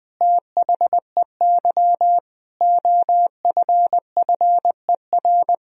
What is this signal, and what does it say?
Morse code, heard as a keyed tone, about 21 wpm, 700 Hz. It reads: THEY OFFER
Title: Sentences from Top 1000 Words - Review Part 4 - 20wpm